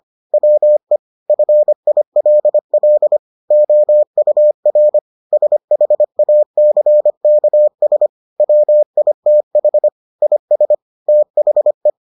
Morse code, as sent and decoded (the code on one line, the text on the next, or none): .-- . / ..-. .. .-.. .-.. / --- ..- .-. / ... .... .- -.-. -.- ... / .-- .. - .... / .. ... / - .... .
WE FILL OUR SHACKS WITH IS THE